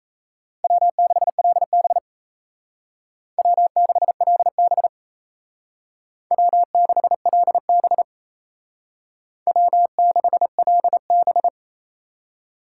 Morse code, heard as a keyed tone, about 36 wpm, 700 Hz. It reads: W6LB W6LB W6LB W6LB